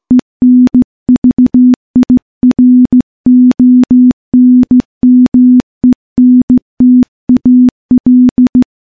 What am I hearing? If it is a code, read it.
Morse code, heard as a keyed tone, about 15 wpm, 250 Hz. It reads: ENVIRONMENTAL
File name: morse / morse-training-15WPM